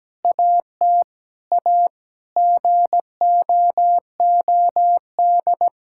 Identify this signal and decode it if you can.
Morse code, heard as a keyed tone, about 17 wpm, 700 Hz. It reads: AT A GOOD
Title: Sentences from Top 1000 Words - Review Part 6 - 17wpm